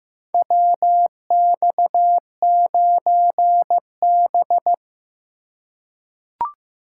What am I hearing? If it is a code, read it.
Morse code, heard as a keyed tone, about 15 wpm, 700 Hz. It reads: WX9B
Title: USA Call Signs - 4 Characters - 2x Character Spacing - 15wpm